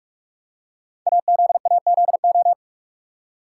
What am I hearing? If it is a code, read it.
Morse code, heard as a keyed tone, about 44 wpm, 690 Hz. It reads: AZAZO
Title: Code Groups - 5 Letters - 45wpm